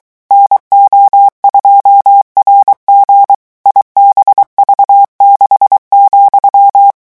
Morse code, heard as a keyed tone, about 23 wpm, 800 Hz. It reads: NO2RG IBV6,
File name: morse mp3 / revision13-vitesse 22 mots